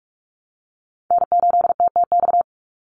Morse code, heard as a keyed tone, about 44 wpm, 700 Hz. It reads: N8TTX
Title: USA Call Signs 45wpm